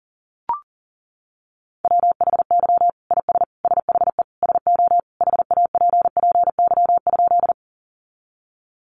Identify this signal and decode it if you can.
Morse code, heard as a keyed tone, about 40 wpm, 700 Hz. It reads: WHY IS SHE SO HAPPY?
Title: Sentences from Top 1000 Words 40wpm